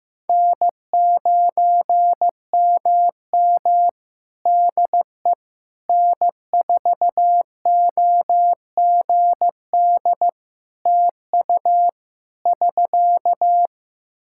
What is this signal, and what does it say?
Morse code, heard as a keyed tone, about 15 wpm, 700 Hz. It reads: N9MM DE N4OGD TU <SK>
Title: K1USN SST QSO Practice 15wpm